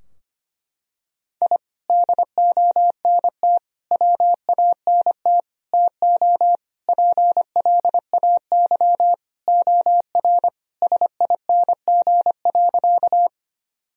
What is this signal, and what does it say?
Morse code, heard as a keyed tone, about 25 wpm, 700 Hz. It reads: I DONT WANT TO PLAY OR SING.
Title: Sentences from Top 1000 Words - Review Part 2 - 25wpm